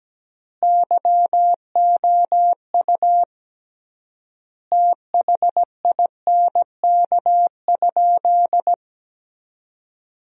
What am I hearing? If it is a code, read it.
Morse code, heard as a keyed tone, about 17 wpm, 700 Hz. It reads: YOU THINK?
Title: 3x Word-Spacing - Sentences from Top 200 Words - 17wpm